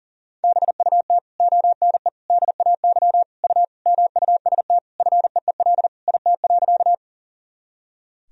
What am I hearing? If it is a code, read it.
Morse code, heard as a keyed tone, about 37 wpm, 700 Hz. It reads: BUT ONE DAY U MUST FEEL IT.
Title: Sentences from Top 400 Words and Abbreviations 40wpm